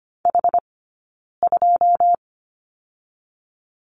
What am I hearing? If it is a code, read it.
Morse code, heard as a keyed tone, about 25 wpm, 700 Hz. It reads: H 2
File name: course / 25:10wpm / 092 Lesson 16d - Letter c - Two Characters 25wpm